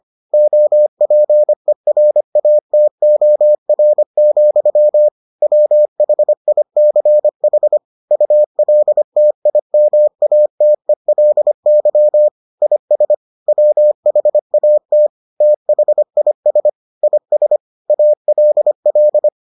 Morse code, heard as a keyed tone, about 25 wpm, 600 Hz. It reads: OPERATOR, WHICH ULTIMATELY IS WHAT THIS IS ALL